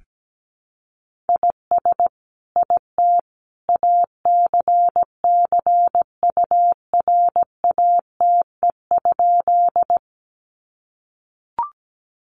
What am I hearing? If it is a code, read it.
Morse code, heard as a keyed tone, about 17 wpm, 700 Hz. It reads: IS IT ACCURATE?
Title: Sentences from Top 1200 Words 17wpm